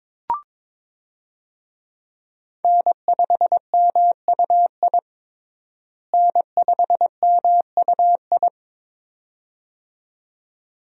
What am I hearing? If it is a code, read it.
Morse code, heard as a keyed tone, about 22 wpm, 700 Hz. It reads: N5MUI N5MUI
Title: USA Call Signs - 5 Characters - Repeated 2x - 22wpm